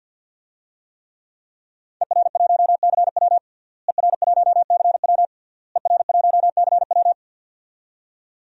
Morse code, heard as a keyed tone, about 49 wpm, 700 Hz. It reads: ER1CW ER1CW ER1CW
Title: DXCC-100 Call Signs - 5 Characters - Repeated 3x - 50wpm